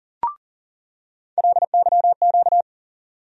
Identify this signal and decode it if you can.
Morse code, heard as a keyed tone, about 40 wpm, 700 Hz. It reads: LYQ
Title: USA Call Sign Postfixes - 3 Letters 40wpm